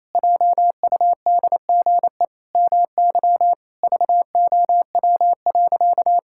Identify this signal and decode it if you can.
Morse code, heard as a keyed tone, about 28 wpm, 700 Hz. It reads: JUDGE MY VOW.
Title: Pangram Challenge v2 - 28wpm